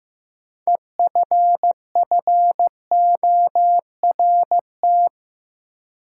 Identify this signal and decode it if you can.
Morse code, heard as a keyed tone, about 15 wpm, 700 Hz. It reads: EFFORT
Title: Sentences from Top 1200 Words - New Word Drill - 15wpm